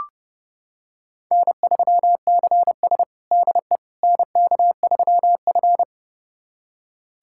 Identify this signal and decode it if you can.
Morse code, heard as a keyed tone, about 32 wpm, 700 Hz. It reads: N3CS DE NK3F
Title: USA Callsign DE Callsign - 4 Characters - 30wpm